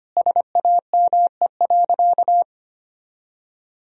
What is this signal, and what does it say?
Morse code, heard as a keyed tone, about 25 wpm, 700 Hz. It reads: SAME.
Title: Sentences from Top 1000 Words - Review Part 3 - 25wpm